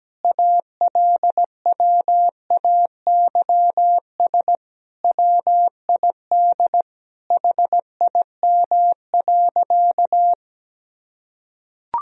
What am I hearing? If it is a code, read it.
Morse code, heard as a keyed tone, about 17 wpm, 700 Hz. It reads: ALWAYS WID HIM.
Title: Sentences from Top 400 Words and Abbreviations 17wpm